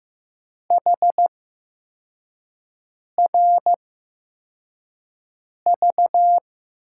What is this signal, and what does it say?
Morse code, heard as a keyed tone, about 15 wpm, spaced as for 4 wpm, 700 Hz. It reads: HRV
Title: USA Call Signs - 8x Character Spacing - 15wpm